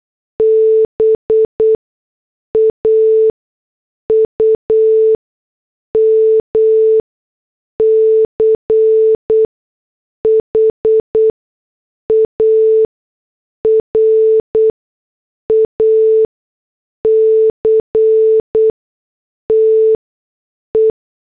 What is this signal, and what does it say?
Morse code, heard as a keyed tone, about 8 wpm, spaced as for 6 wpm, 440 Hz. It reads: BAUM CHARACTE